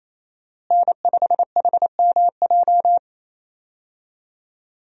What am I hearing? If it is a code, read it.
Morse code, heard as a keyed tone, about 28 wpm, 700 Hz. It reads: N5HMJ
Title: USA Call Signs - 5 Characters - Repeated 3x - 28wpm